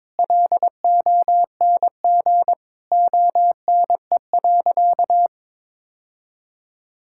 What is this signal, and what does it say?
Morse code, heard as a keyed tone, about 22 wpm, 700 Hz. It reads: LONG ONE.
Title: Sentences from Top 1000 Words - Review Part 5 - 22wpm